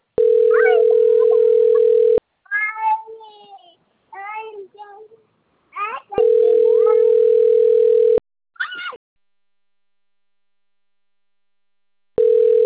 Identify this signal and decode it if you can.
Ringback tone (North America), 440 + 480 Hz.